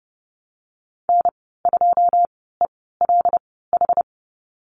Morse code, heard as a keyed tone, about 30 wpm, spaced as for 17 wpm, 700 Hz. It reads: N2ELH
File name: Lesson 14g - Number 2 - Call Signs 30wpm